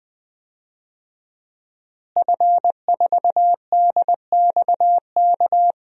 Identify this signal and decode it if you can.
Morse code, heard as a keyed tone, about 20 wpm, 700 Hz. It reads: F4DXK
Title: DXCC-100 Call Signs - 5 Characters - Repeated 2x - 20wpm